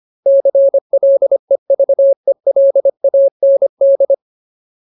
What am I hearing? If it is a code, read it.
Morse code, heard as a keyed tone, about 25 wpm, 550 Hz. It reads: CLEVELAND
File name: Ultimate Word List / 25WPM Ultimate